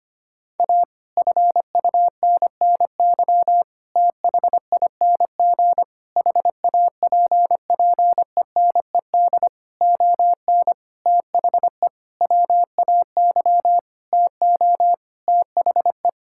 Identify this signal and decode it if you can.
Morse code, heard as a keyed tone, about 25 wpm, 700 Hz. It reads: A FUNNY THING HAPPENED ON THE WAY TO THE
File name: practice / 25wpm / Sentences with 9-Word Phrases - 25wpm